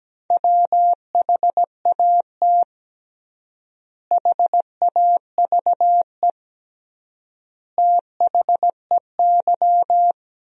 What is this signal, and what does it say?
Morse code, heard as a keyed tone, about 17 wpm, 700 Hz. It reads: WHAT HAVE THEY
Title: 3x Word-Spacing - Sentences from Top 500 Words - 17wpm